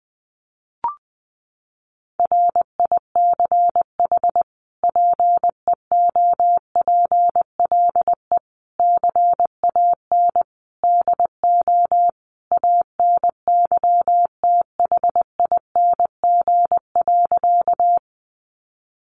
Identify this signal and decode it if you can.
Morse code, heard as a keyed tone, about 20 wpm, 700 Hz. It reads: RICH PEOPLE CAN DO ANYTHING.
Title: Sentences from Top 1100 Words 20wpm